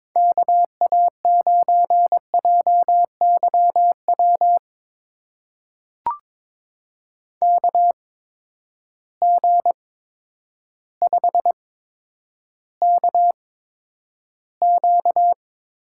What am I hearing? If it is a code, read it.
Morse code, heard as a keyed tone, about 22 wpm, spaced as for 10 wpm, 700 Hz. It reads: KA9JYW KG5KQ
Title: USA Call Signs - 6 Characters - 8x Character Spacing - 22wpm